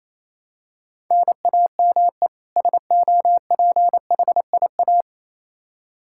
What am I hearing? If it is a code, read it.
Morse code, heard as a keyed tone, about 28 wpm, 700 Hz. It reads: NAME SOPHIA
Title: Practice QSO - Exchange 1 - Beginning - 28wpm